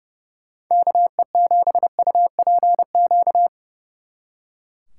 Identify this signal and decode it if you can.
Morse code, heard as a keyed tone, about 36 wpm, 700 Hz. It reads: KE7UPQ
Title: Speed Racing - USA 2x3 Call Signs 30wpm